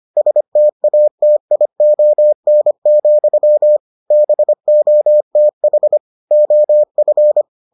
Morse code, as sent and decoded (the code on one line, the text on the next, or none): ... - .- - .. --- -. --..-- / -... --- - .... / --- ..-.
STATION, BOTH OF